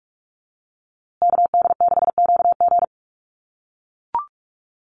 Morse code, heard as a keyed tone, about 44 wpm, 700 Hz. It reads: KD6QG
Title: USA Call Signs Repeated 3x 45wpm